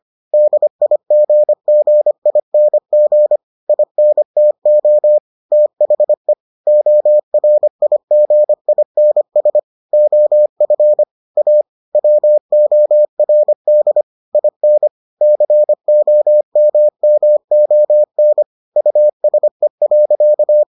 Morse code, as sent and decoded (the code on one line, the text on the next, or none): -.. .. --. --. .. -. --. / .. -. - --- / - .... / --- .-. .. --. .. -. ... / --- ..-. / .- / .-- --- .-. -.. / .. -. / -.-. --- -- -- --- -. / ..- ... .-.-.-
DIGGING INTO THE ORIGINS OF A WORD IN COMMON USE.